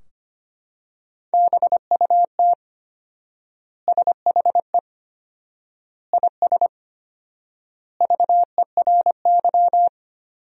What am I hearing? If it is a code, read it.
Morse code, heard as a keyed tone, about 25 wpm, 700 Hz. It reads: BUT SHE IS VERY
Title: 4x Word-Spacing - Sentences from Top 400 Words - 25wpm